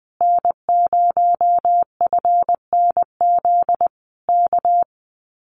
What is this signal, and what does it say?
Morse code, heard as a keyed tone, about 20 wpm, 700 Hz. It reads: N0FNZ K
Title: Common QSO Phrases Encore 20wpm